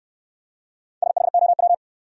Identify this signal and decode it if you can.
Morse code, heard as a keyed tone, about 57 wpm, 700 Hz. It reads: SHOW